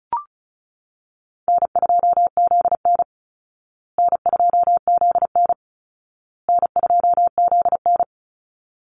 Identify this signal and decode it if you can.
Morse code, heard as a keyed tone, about 34 wpm, 700 Hz. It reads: N2ZN N2ZN N2ZN
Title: Call Signs Repeated 2x - 35wpm